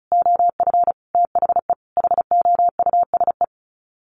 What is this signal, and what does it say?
Morse code, heard as a keyed tone, about 34 wpm, 700 Hz. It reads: OF THE HOUSE
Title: Sets of 5 Words Encore2 35wpm